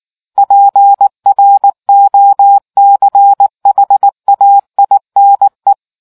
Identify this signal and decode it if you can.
Morse code, heard as a keyed tone, about 19 wpm, 800 Hz. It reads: PROCHAINE